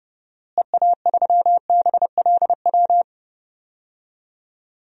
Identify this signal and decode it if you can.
Morse code, heard as a keyed tone, about 30 wpm, 700 Hz. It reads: EA3BLW
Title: DXCC-100 Call Signs - 6 Characters - Repeated 3x - 30wpm